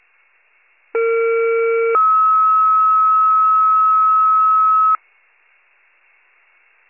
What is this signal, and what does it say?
Two-tone page, heard as a one-second tone, then a three-second tone, about 460 then 1290 Hz. It is heard from 0.9 to 5.0 s.
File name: pager / 21 RESCUE / wcalert